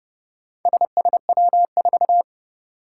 Morse code, heard as a keyed tone, about 30 wpm, 690 Hz. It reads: SSW4
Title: Four Challenging Letter-Number Combinations 30wpm